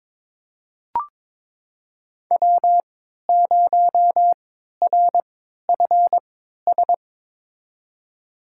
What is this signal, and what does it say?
Morse code, heard as a keyed tone, about 22 wpm, spaced as for 12 wpm, 700 Hz. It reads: W0RFS
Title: USA Call Signs - 3x Character Spacing - 22wpm